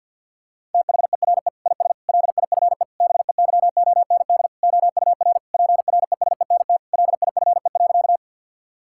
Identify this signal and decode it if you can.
Morse code, heard as a keyed tone, about 52 wpm, 700 Hz. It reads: THERE IS LIFE BEYOND OUR PRESENT LIFE.
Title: Sentences from Top 1000 Words - Review Part 6 - 50wpm